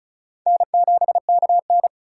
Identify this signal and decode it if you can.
Morse code, heard as a keyed tone, about 34 wpm, 690 Hz. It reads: N7KN
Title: USA Call Signs - 4 Characters - 2x Character Spacing - 35wpm